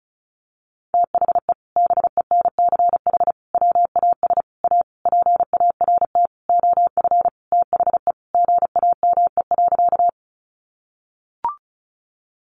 Morse code, heard as a keyed tone, about 34 wpm, 700 Hz. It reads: THE BENCH WAS A PART OF THE GAME.
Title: Sentences from Top 1400 Words 35wpm